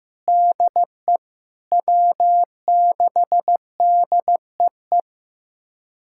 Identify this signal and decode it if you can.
Morse code, heard as a keyed tone, about 15 wpm, 700 Hz. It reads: DE W6DEE
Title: USA Callsign DE Callsign - 5 Characters - 15wpm